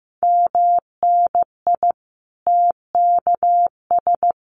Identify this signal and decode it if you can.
Morse code, heard as a keyed tone, about 15 wpm, 700 Hz. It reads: MNI TKS
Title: Common QSO Phrases 15wpm